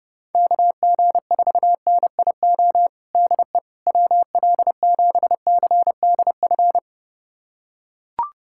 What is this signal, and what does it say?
Morse code, heard as a keyed tone, about 30 wpm, 700 Hz. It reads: KG4NIO DE WL7CDF
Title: USA Callsign DE Callsign - 6 Characters - 30wpm